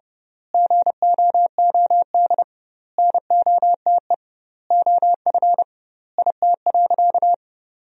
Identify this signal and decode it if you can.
Morse code, heard as a keyed tone, about 30 wpm, 700 Hz. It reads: GOOD NOTE OF IT.
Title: 2x Word-Spacing - Sentences from Top 500 Words - 30wpm